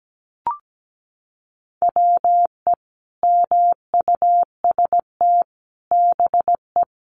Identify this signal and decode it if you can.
Morse code, heard as a keyed tone, about 17 wpm, 700 Hz. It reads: WE MUST BE
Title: Sentences from Top 500 Words 17wpm